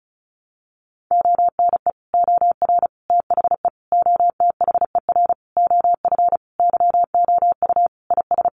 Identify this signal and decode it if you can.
Morse code, heard as a keyed tone, about 34 wpm, 700 Hz. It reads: ONE OR THE OTHER OF YOU IS
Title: Sentences from Top 400 Words 35wpm